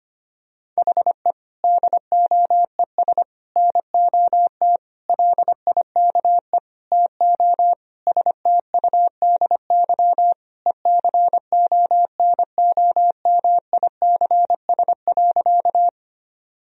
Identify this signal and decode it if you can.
Morse code, heard as a keyed tone, about 25 wpm, 700 Hz. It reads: HE DOES NOT LIKE TO STUDY ECONOMICS.
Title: Sentences from Top 1000 Words - Review Part 3 - 25wpm